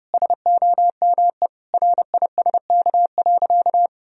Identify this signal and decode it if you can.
Morse code, heard as a keyed tone, about 30 wpm, 700 Hz. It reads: SOME RISK.
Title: Sentences from Top 1000 Words - Review Part 4 - 30wpm